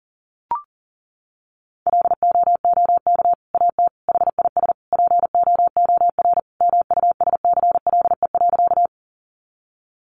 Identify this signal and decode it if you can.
Morse code, heard as a keyed tone, about 40 wpm, 700 Hz. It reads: LOOK AT HIS POOR MUSCLE.
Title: Sentences from Top 1300 Words 40wpm